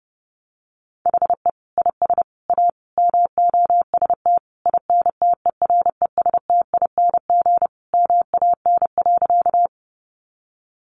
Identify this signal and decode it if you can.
Morse code, heard as a keyed tone, about 30 wpm, 700 Hz. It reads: HE IS A MOST INTERESTING MAN.
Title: Sentences from Top 1000 Words 30wpm